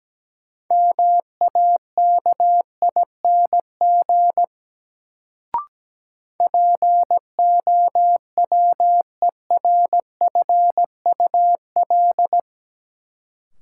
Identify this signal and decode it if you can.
Morse code, heard as a keyed tone, about 17 wpm, 700 Hz. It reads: MAKING POWERFUL
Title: Sentences from Top 1000 Words - Review Part 6 - New Word Drill - 17wpm